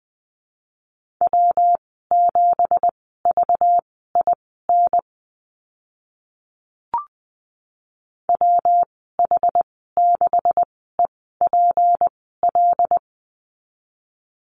Morse code, heard as a keyed tone, about 20 wpm, spaced as for 15 wpm, 700 Hz. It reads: W7VIN WH6EPL
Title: 193 Lesson 31g - Letter v - Call Signs 20wpm